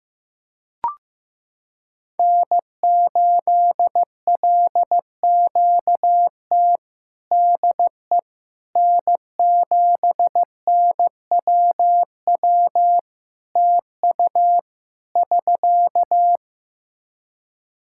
Morse code, heard as a keyed tone, about 15 wpm, 700 Hz. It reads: N8LQT DE N7NWW TU <SK>
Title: Practice QSO - Exchange 4 - Ending - 15wpm